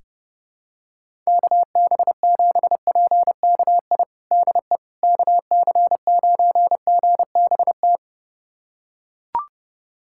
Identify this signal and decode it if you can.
Morse code, heard as a keyed tone, about 32 wpm, 700 Hz. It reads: KB7PKI DE KC9GBT
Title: USA Callsign DE Callsign - 6 Characters - 30wpm